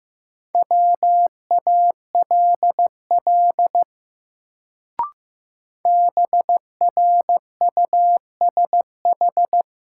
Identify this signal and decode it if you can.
Morse code, heard as a keyed tone, about 15 wpm, 700 Hz. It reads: WALL BRUSH
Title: Sentences from Top 1100 Words - New Word Drill Alt - 15wpm